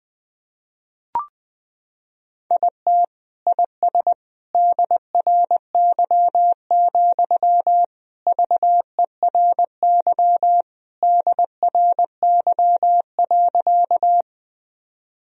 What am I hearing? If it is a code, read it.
Morse code, heard as a keyed tone, about 21 wpm, 700 Hz. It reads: IT IS DRY, VERY DRY.
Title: Sentences from Top 1000 Words - Review Part 3 - 20wpm